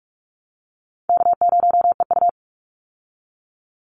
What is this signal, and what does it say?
Morse code, heard as a keyed tone, about 44 wpm, 700 Hz. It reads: K0EU
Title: Call Signs Repeated 3x - 45wpm